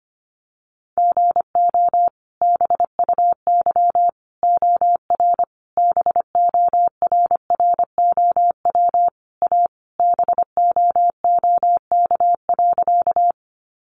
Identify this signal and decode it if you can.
Morse code, heard as a keyed tone, about 25 wpm, 700 Hz. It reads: GO BUY OR BORROW A BOOK.
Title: Sentences from Top 1500 Words 25wpm